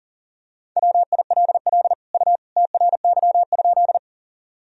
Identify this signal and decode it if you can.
Morse code, heard as a keyed tone, about 40 wpm, 700 Hz. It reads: WILL U TRY?
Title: Sentences from Top 200 Words and Abbreviations 40wpm